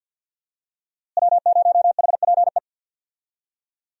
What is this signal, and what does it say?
Morse code, heard as a keyed tone, about 52 wpm, 700 Hz. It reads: W0HPE
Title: USA Call Signs - 5 Characters - Repeated 3x - 50wpm